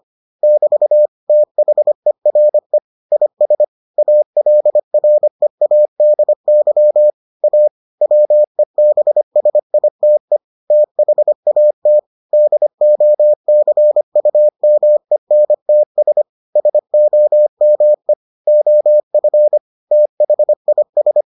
Morse code, heard as a keyed tone, about 25 wpm, 600 Hz. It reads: <BT> THERE IS ALREADY A WEBSITE THAT DOCUMENTS SOME OF THIS